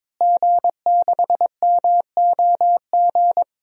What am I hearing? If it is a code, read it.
Morse code, heard as a keyed tone, about 22 wpm, 700 Hz. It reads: G6MOG
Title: DXCC-100 Call Signs - 5 Characters - Repeated 3x - 22wpm